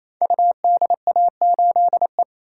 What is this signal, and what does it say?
Morse code, heard as a keyed tone, about 28 wpm, 700 Hz. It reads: UDA8E
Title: Code Groups - 5 Letter:Numbers - 28wpm